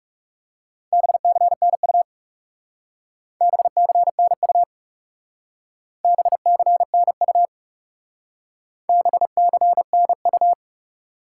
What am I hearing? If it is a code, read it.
Morse code, heard as a keyed tone, about 35 wpm, 700 Hz. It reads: BCNU BCNU BCNU BCNU
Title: Speed-Racing Top 90 QSO Elements 30wpm